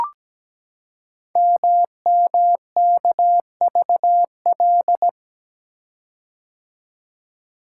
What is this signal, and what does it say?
Morse code, heard as a keyed tone, about 17 wpm, 700 Hz. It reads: MMKVL
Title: Code Groups - 5 Letters - 17wpm